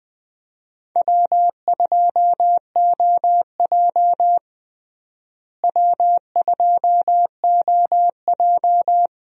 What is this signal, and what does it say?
Morse code, heard as a keyed tone, about 21 wpm, 700 Hz. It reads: W2OJ W2OJ
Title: USA Call Signs - 4 Characters - Repeated 3x - 20wpm